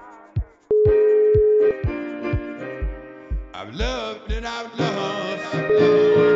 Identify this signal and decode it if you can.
Ringback tone (Europe, 425 Hz).